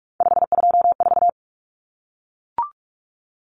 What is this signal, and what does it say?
Morse code, heard as a keyed tone, about 44 wpm, 700 Hz. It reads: 524
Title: Common Cut Numbers - 3-Digit 45wpm